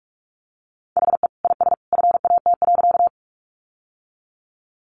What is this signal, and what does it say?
Morse code, heard as a keyed tone, about 44 wpm, 700 Hz. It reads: HE IS FAT.
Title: Sentences from Top 600 Words 45wpm